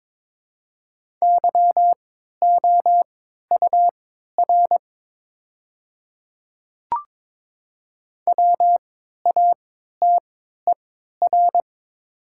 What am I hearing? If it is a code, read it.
Morse code, heard as a keyed tone, about 22 wpm, spaced as for 12 wpm, 700 Hz. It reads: YOUR WATER
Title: 3x Character Spacing - Top 200 Words - 22wpm